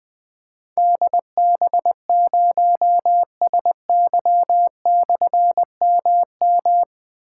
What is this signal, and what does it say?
Morse code, heard as a keyed tone, about 20 wpm, 700 Hz. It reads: DB0SY/MM
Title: Difficult Call Signs 20wpm